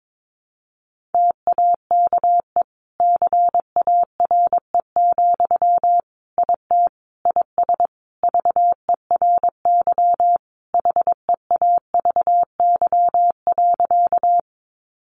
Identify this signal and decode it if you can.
Morse code, heard as a keyed tone, about 22 wpm, 700 Hz. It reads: TAKE CARE, IT IS VERY HEAVY.
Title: Sentences from Top 1100 Words 22wpm